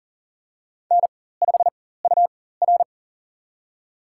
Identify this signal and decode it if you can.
Morse code, heard as a keyed tone, about 40 wpm, spaced as for 19 wpm, 700 Hz. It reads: N5UR
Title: USA Call Signs - 4 Characters - 4x Character Spacing - 40wpm